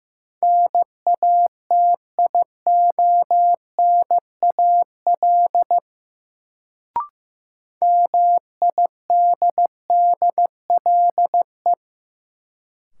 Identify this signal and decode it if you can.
Morse code, heard as a keyed tone, about 16 wpm, 700 Hz. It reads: NATIONAL MIDDLE
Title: Sets of 2 Words - New Word Drill - 15wpm